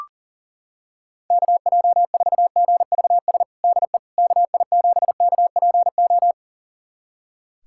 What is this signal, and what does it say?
Morse code, heard as a keyed tone, about 40 wpm, 700 Hz. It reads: KJ4GVS DE KI7KPO
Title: USA Callsign DE Callsign - 6 Characters - 40wpm